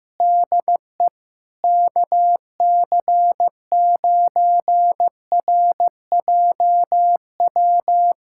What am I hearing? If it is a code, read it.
Morse code, heard as a keyed tone, about 15 wpm, 700 Hz. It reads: DE KC9RJW
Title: USA Callsign DE Callsign - 6 Characters - 15wpm